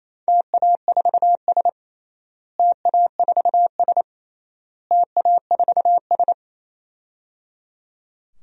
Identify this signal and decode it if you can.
Morse code, heard as a keyed tone, about 28 wpm, 700 Hz. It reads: TA4S TA4S TA4S
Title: DXCC-100 Call Signs - 4 Characters - Repeated 3x - 28wpm